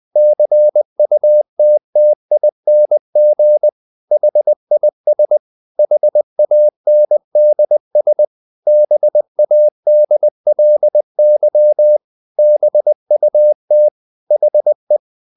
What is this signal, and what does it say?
Morse code, heard as a keyed tone, about 20 wpm, 600 Hz. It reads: CUTTING HIS HANDS BADLY BUT HE
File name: the old man and the sea part 2 20wpm